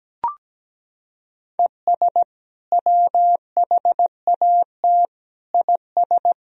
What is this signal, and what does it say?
Morse code, heard as a keyed tone, about 17 wpm, 700 Hz. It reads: ES WHAT IS
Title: Sentences from Top 100 Words and Abbreviations 17wpm